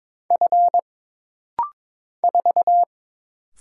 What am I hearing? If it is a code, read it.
Morse code, heard as a keyed tone, about 22 wpm, 700 Hz. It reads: F 4
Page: This rapid-fire timing is geared to push your ICR (Instant Character Recognition) skills.